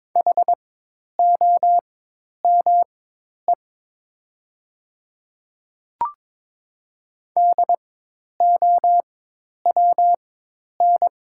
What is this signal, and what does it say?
Morse code, heard as a keyed tone, about 22 wpm, spaced as for 10 wpm, 700 Hz. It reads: HOME DOWN
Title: 4x Character Spacing - Top 200 Words - 22wpm